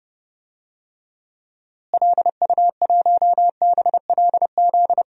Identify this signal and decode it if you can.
Morse code, heard as a keyed tone, about 30 wpm, 700 Hz. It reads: LU1BLZ
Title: DXCC-100 Call Signs - 6 Characters - Repeated 3x - 30wpm